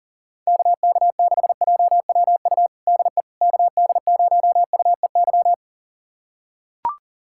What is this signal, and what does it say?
Morse code, heard as a keyed tone, about 40 wpm, 700 Hz. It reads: KK6JWU DE KD0UEY
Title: USA Callsign DE Callsign 40wpm